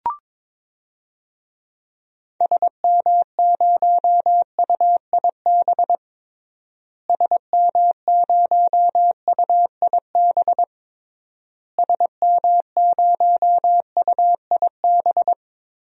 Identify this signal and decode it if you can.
Morse code, heard as a keyed tone, about 22 wpm, 700 Hz. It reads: SM0UIB SM0UIB SM0UIB
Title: DXCC-100 Call Signs - 6 Characters - Repeated 3x - 22wpm